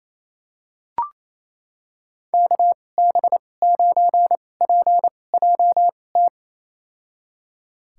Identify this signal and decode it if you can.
Morse code, heard as a keyed tone, about 28 wpm, spaced as for 20 wpm, 700 Hz. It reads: KB9PJT
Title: USA Call Signs - 2x Character Spacing - 28wpm